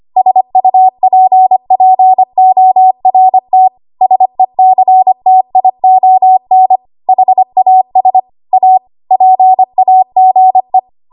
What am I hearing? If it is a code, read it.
Morse code, heard as a keyed tone, about 25 wpm, 750 Hz. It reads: SUPPORT SECTION HAS A PAGE
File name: Archive / 25 WPM / 25WPM